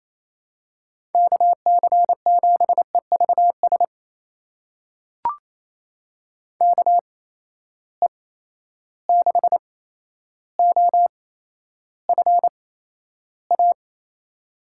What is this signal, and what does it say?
Morse code, heard as a keyed tone, about 28 wpm, spaced as for 12 wpm, 700 Hz. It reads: KC7EVS KE6OFA
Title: USA Call Signs - 6 Characters - 8x Character Spacing - 28wpm